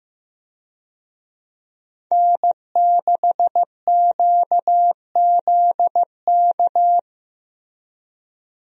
Morse code, heard as a keyed tone, about 16 wpm, 700 Hz. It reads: N6QZK
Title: USA Call Signs - 5 Characters - Repeated 3x - 15wpm